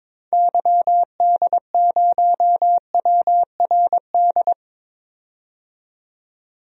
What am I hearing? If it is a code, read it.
Morse code, heard as a keyed tone, about 21 wpm, 700 Hz. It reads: YD0WRD
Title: DXCC-100 Call Signs - 6 Characters - 22wpm